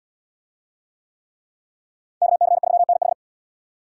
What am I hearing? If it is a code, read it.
Morse code, heard as a keyed tone, about 70 wpm, 700 Hz. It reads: KC3NH